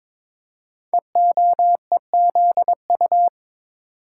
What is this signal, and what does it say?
Morse code, heard as a keyed tone, about 22 wpm, 700 Hz. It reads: EOEZU
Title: Code Groups - 5 Letters - 22wpm